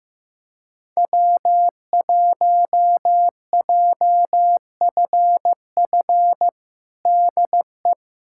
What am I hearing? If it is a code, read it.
Morse code, heard as a keyed tone, about 15 wpm, 700 Hz. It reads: W1JFF DE
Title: USA Callsign DE Callsign 15wpm